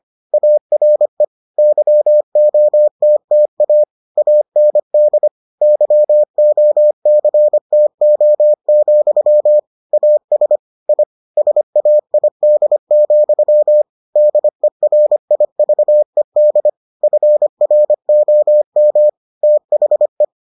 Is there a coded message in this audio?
Yes: Morse code from 0.3 to 20.2 s, heard as a keyed tone, about 25 wpm, 600 Hz. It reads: ARE YOTTA AND YOCTO, AS I SAID, DERIVED FROM THE